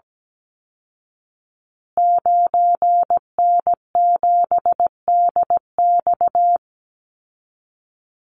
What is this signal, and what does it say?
Morse code, heard as a keyed tone, about 17 wpm, 700 Hz. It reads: 9N7DX
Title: Call Signs Repeated 3x - 17wpm